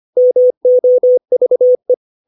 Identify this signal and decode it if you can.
Morse code, heard as a keyed tone, about 25 wpm, 500 Hz. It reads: MOVE
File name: 632 words / 632 Words 25WPM